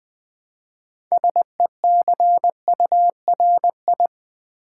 Morse code, heard as a keyed tone, about 20 wpm, 700 Hz. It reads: SECURI
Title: Build A Word - 3 Letters At A Time - 8-Letter Words - 20wpm